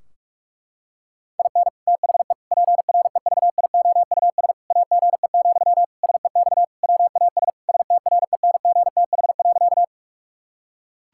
Morse code, heard as a keyed tone, about 41 wpm, 700 Hz. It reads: IN THE PREVIOUS AGE, SEX WAS STRENGTH.
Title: Sentences from Top 1600 Words 45wpm